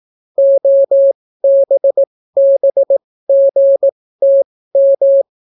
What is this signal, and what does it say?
Morse code, heard as a keyed tone, about 18 wpm, spaced as for 14 wpm, 550 Hz. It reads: OBBGTM